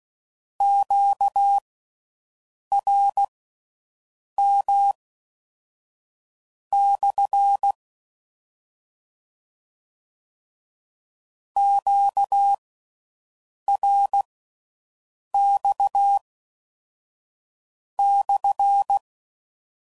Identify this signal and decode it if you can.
Morse code, heard as a keyed tone, about 16 wpm, spaced as for 5 wpm, 800 Hz. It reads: QRM/ QRX/